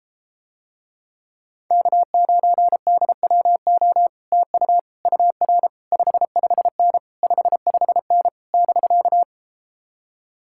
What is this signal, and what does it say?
Morse code, heard as a keyed tone, about 33 wpm, 700 Hz. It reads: K9DWO TU UR 55N 55N <BK>